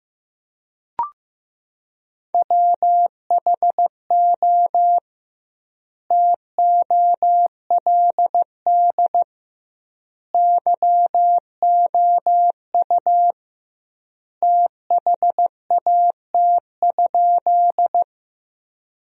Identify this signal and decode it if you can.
Morse code, heard as a keyed tone, about 15 wpm, 700 Hz. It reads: WHO TOLD YOU THAT?
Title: 2x Word-Spacing - Sentences from Top 400 Words - 15wpm